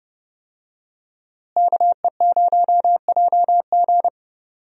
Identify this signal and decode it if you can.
Morse code, heard as a keyed tone, about 30 wpm, 700 Hz. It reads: KE0JG
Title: USA Call Signs - 5 Characters - Repeated 3x - 30wpm